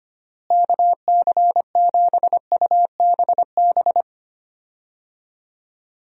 Morse code, heard as a keyed tone, about 25 wpm, 700 Hz. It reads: KC7UBB
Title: USA Call Signs - 6 Characters - Repeated 2x - 25wpm